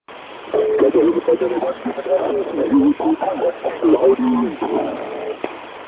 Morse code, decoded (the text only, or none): none